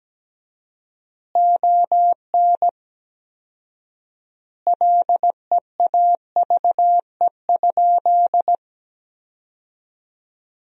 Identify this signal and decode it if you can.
Morse code, heard as a keyed tone, about 17 wpm, 700 Hz. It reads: ON LEAVE?
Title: 4x Word-Spacing - Sentences from Top 400 Words - 17wpm